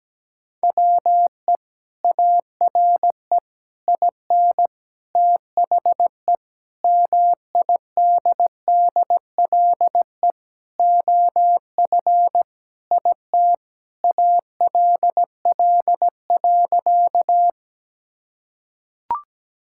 Morse code, heard as a keyed tone, about 17 wpm, 700 Hz. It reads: WE ARE IN THE MIDDLE OF IT ALL.
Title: Sentences from Top 1000 Words - Review Part 6 - 17wpm